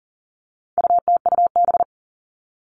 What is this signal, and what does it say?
Morse code, heard as a keyed tone, about 40 wpm, 700 Hz. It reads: UTUB
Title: All Cut Numbers - 4-Digit 40wpm